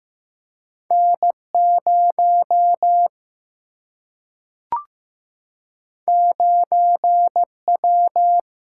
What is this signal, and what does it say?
Morse code, heard as a keyed tone, about 15 wpm, 700 Hz. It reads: N0 9W
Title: Single-Letter Single-Number 15wpm